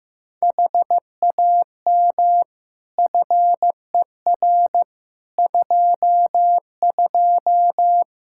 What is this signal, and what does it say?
Morse code, heard as a keyed tone, about 15 wpm, 700 Hz. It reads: HAM FER 22
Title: Practice QSO - Exchange 3 - More Info - 15wpm